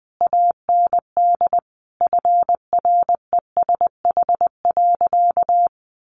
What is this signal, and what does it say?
Morse code, heard as a keyed tone, about 20 wpm, 700 Hz. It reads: AND FRESH.
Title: Sentences from Top 1200 Words 20wpm